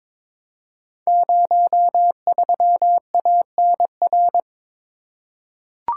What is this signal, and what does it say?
Morse code, heard as a keyed tone, about 22 wpm, 700 Hz. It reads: 03ANR
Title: Code Groups - 5 Letter:Numbers - 22wpm